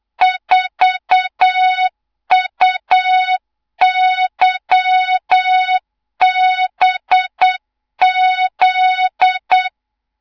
Morse code, heard as a keyed tone, about 8 wpm, 3700 Hz, with noise. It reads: 4UYBZ
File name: Morze-HA2MN / hang / 24 GYAKORLAT